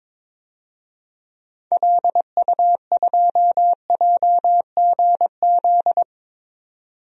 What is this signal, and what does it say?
Morse code, heard as a keyed tone, about 22 wpm, 700 Hz. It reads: LU2JGZ
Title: DXCC-100 Call Signs - 6 Characters - 22wpm